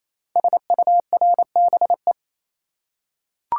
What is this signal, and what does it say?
Morse code, heard as a keyed tone, about 27 wpm, 700 Hz. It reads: SURBE
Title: Code Groups - 5 Letters - 28wpm